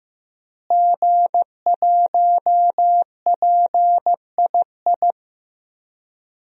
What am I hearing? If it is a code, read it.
Morse code, heard as a keyed tone, about 15 wpm, 700 Hz. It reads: G1PII
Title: DXCC-100 Call Signs - 5 Characters - Repeated 3x - 15wpm